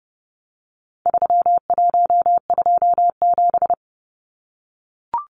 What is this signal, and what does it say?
Morse code, heard as a keyed tone, about 30 wpm, 700 Hz. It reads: 3127
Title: Numbers - 4-Digit 30wpm